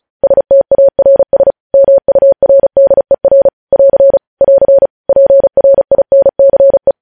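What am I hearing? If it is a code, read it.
Morse code, heard as a keyed tone, about 36 wpm, 550 Hz. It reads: STARS MURDER <AR> <AR> PRINCE
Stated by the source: Morse code news headlines at 35.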